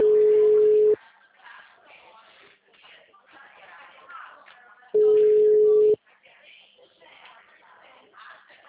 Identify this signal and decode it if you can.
Ringback tone (Europe, 425 Hz), with speech.
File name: out- -2020- - - .352